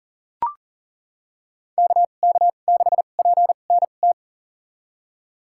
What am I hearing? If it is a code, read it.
Morse code, heard as a keyed tone, about 40 wpm, spaced as for 29 wpm, 700 Hz. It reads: KK6PNT
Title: USA Call Signs - 2x Character Spacing - 40wpm